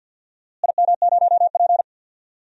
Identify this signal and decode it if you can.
Morse code, heard as a keyed tone, about 50 wpm, 700 Hz. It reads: IM0P
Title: DXCC-100 Call Signs - 4 Characters - Repeated 3x - 50wpm